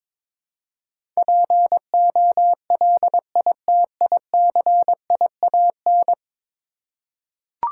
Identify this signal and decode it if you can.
Morse code, heard as a keyed tone, about 22 wpm, 700 Hz. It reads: POLITICIAN
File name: practice / 22wpm / 10 Letter Words - First 3 Letters Given - 22wpm